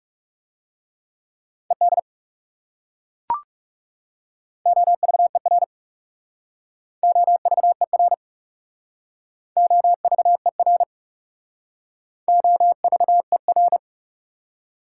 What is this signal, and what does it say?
Morse code, heard as a keyed tone, about 42 wpm, 690 Hz. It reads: ED OVER OVER OVER OVER